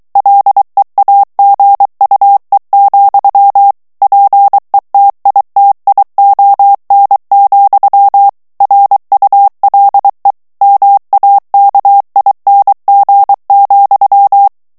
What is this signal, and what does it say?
Morse code, heard as a keyed tone, about 23 wpm, 800 Hz. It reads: LEAGUE, PETITION, RULE MAKING,
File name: ANGLAIS texte clair 3 vitesse 22 mots minute